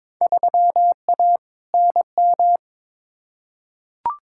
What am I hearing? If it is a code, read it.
Morse code, heard as a keyed tone, about 22 wpm, 700 Hz. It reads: 3A NM
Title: Field Day Exchanges Encore - 22wpm